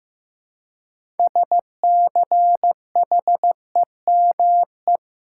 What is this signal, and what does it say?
Morse code, heard as a keyed tone, about 15 wpm, 700 Hz. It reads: SCHEME
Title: Six Letter Words - 15wpm